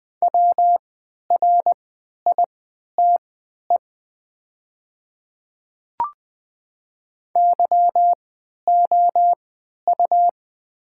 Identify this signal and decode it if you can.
Morse code, heard as a keyed tone, about 20 wpm, spaced as for 11 wpm, 700 Hz. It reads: WRITE YOU
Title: 3x Character Spacing - Top 300 Words - 20wpm